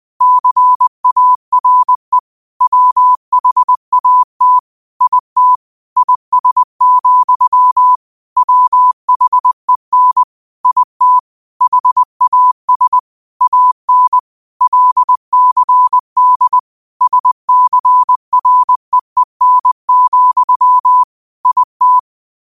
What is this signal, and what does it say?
Morse code, heard as a keyed tone, about 20 wpm, 1000 Hz. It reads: CARE WHAT IT IS, WHEN IT HAS AN LCD SCREEN, IT
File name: qotdfiles / 20wpm